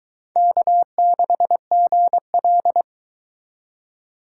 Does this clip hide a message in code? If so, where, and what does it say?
Yes: Morse code from 0.4 to 2.8 s, heard as a keyed tone, about 23 wpm, 700 Hz. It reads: K6GL